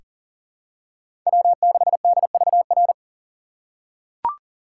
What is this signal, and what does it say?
Morse code, heard as a keyed tone, about 37 wpm, 700 Hz. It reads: W6DVR
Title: USA Call Signs - 5 Characters - 4x Character Spacing - 40wpm